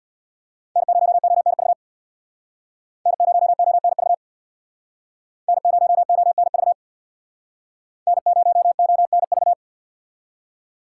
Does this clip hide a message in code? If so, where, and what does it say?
Yes: Morse code from 0.8 to 9.5 s, heard as a keyed tone, about 63 wpm, 700 Hz. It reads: N0ONV N0ONV N0ONV N0ONV